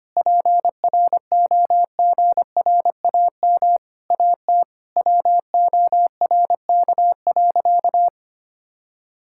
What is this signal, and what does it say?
Morse code, heard as a keyed tone, about 25 wpm, 700 Hz. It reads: PROGRAM AT WORK.
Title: Sentences from Top 1000 Words - Review Part 5 - 25wpm